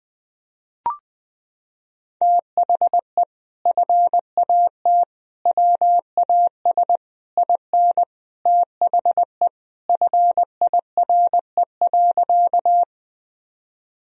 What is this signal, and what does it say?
Morse code, heard as a keyed tone, about 20 wpm, 700 Hz. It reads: THE FAT WAS IN THE FIRE.
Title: Sentences from Top 1000 Words - Review Part 4 - 20wpm